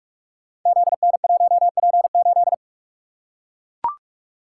Morse code, heard as a keyed tone, about 44 wpm, 700 Hz. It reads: ZN1P8
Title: Code Groups - 5 Letter:Numbers - 45wpm